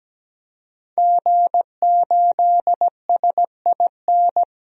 Morse code, heard as a keyed tone, about 17 wpm, 700 Hz. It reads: G8SIN
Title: DXCC-100 Call Signs - 5 Characters - 17wpm